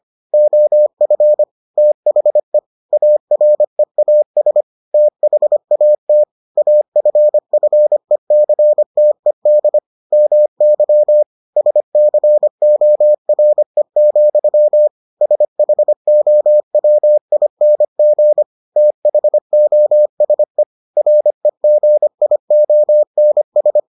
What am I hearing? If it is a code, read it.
Morse code, heard as a keyed tone, about 25 wpm, 600 Hz. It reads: OF THE AREAS THAT AFFECTED MY SCORE, SHOWING THOSE REGIONS